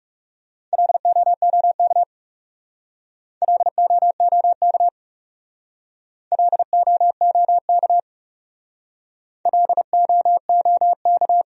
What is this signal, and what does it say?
Morse code, heard as a keyed tone, about 40 wpm, 700 Hz. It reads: LOOK LOOK LOOK LOOK